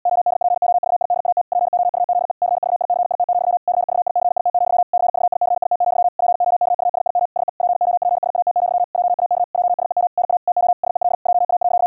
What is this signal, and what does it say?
Morse code, heard as a keyed tone, about 80 wpm, 710 Hz. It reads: ODPOVIDAT POUZE CVICIT, CVICIT, CVICIT, POKUSIM SE POPSAT, JAK JSEM NA TO SEL JA.
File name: hamradio / clanek-qrq-70wpm / 80wpm